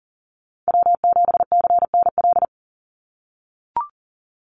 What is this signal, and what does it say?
Morse code, heard as a keyed tone, about 40 wpm, 700 Hz. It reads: W7CNL
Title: NA SOTA Chaser Callsigns 40wpm